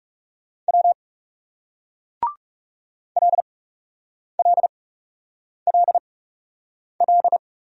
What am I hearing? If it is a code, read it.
Morse code, heard as a keyed tone, about 41 wpm, spaced as for 8 wpm, 700 Hz. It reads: W LLLL